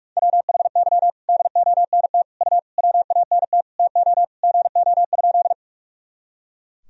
Morse code, heard as a keyed tone, about 46 wpm, 700 Hz. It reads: WHY DONT U WANT TO GO?
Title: Sentences from Top 300 Words and Abbreviations 45wpm